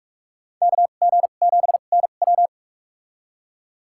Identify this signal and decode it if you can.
Morse code, heard as a keyed tone, about 46 wpm, spaced as for 33 wpm, 700 Hz. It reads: KG7NW
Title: USA Call Signs - 5 Characters - 2x Character Spacing - 45wpm